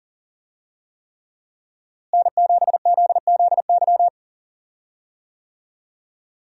Morse code, heard as a keyed tone, about 38 wpm, 700 Hz. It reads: N7ZZY
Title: USA Call Signs - 5 Characters - Repeated 3x - 40wpm